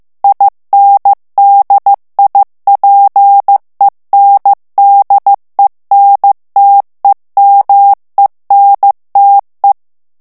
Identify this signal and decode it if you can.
Morse code, heard as a keyed tone, about 15 wpm, 800 Hz. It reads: INDIPENDENTEMENTE
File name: morse mp3 / ITALIEN texte clair 2 vitesse 14 mots minute